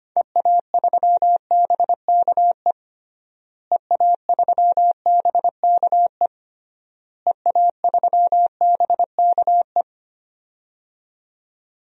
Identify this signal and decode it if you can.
Morse code, heard as a keyed tone, about 25 wpm, 700 Hz. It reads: EA3BKE EA3BKE EA3BKE
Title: DXCC-100 Call Signs - 6 Characters - Repeated 3x - 25wpm